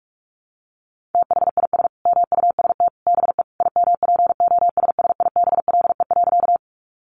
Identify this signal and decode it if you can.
Morse code, heard as a keyed tone, about 44 wpm, 700 Hz. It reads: THIS MUST BE IMPOSSIBLE.
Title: Sentences from Top 1100 Words 45wpm